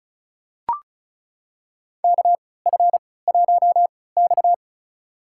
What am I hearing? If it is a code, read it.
Morse code, heard as a keyed tone, about 36 wpm, spaced as for 20 wpm, 700 Hz. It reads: KF1X
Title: USA Call Signs - 4 Characters - 3x Character Spacing - 35wpm